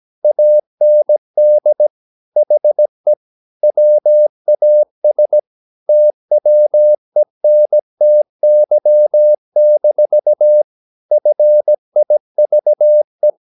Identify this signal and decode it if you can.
Morse code, heard as a keyed tone, about 17 wpm, 600 Hz. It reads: AND HE WAS TWENTY-FIVE
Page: Romanen Never af Ken Follet på engelsk.